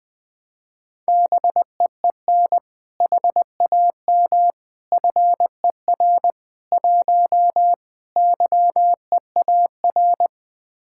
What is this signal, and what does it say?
Morse code, heard as a keyed tone, about 20 wpm, 700 Hz. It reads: BEEN HAM FER 1 YEAR
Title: Speed-Racing Common QSO Phrases - 20wpm